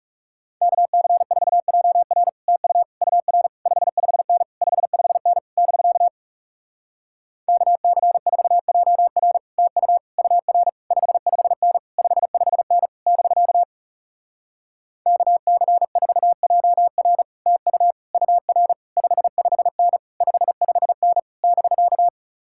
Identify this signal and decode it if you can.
Morse code, heard as a keyed tone, about 38 wpm, 700 Hz. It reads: KC4JR TU UR 55N 55N <BK> KC4JR TU UR 55N 55N <BK> KC4JR TU UR 55N 55N <BK>